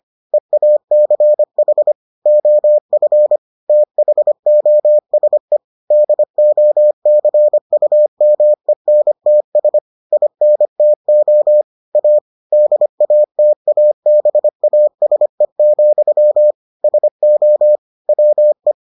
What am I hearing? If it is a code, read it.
Morse code, heard as a keyed tone, about 25 wpm, 600 Hz. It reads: EACH OF THOSE DOCUMENTS INTO A DATABASE, SO WE